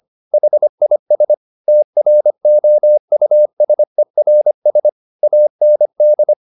Morse code, heard as a keyed tone, about 25 wpm, 600 Hz. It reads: HIS TROUSERS AND
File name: the old man and the sea part 1 25wpm